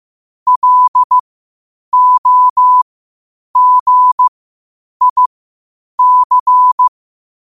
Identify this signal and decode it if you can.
Morse code, heard as a keyed tone, about 15 wpm, spaced as for 8 wpm, 1000 Hz. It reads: LOGIC
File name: qotdfiles / 8wpm